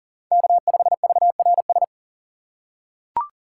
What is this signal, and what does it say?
Morse code, heard as a keyed tone, about 37 wpm, 700 Hz. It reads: K5VRS
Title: USA Call Signs - 5 Characters - 3x Character Spacing - 40wpm